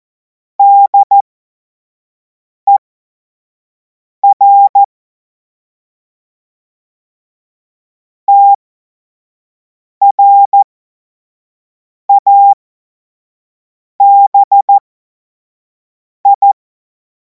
Morse code, heard as a keyed tone, about 14 wpm, spaced as for 5 wpm, 800 Hz. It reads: DER TRABI
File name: Morsekurs / Witze / Witz 21. April 2026 25 F70